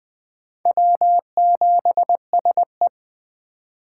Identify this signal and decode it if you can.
Morse code, heard as a keyed tone, about 20 wpm, 700 Hz. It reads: W7SE
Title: K1USN SST QSO Practice - 20wpm